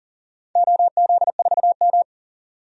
Morse code, heard as a keyed tone, about 40 wpm, 700 Hz. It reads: OZ4M
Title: DXCC-100 Call Signs - 4 Characters - 40wpm